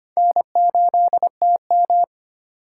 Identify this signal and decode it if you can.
Morse code, heard as a keyed tone, about 25 wpm, 700 Hz. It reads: N8TM
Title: USA Call Signs - 4 Characters - 2x Character Spacing - 25wpm